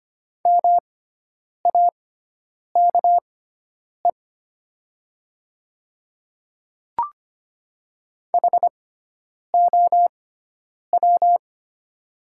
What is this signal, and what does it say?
Morse code, heard as a keyed tone, about 25 wpm, spaced as for 9 wpm, 700 Hz. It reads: MAKE HOW
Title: 6x Character Spacing - Top 100 Words - 25wpm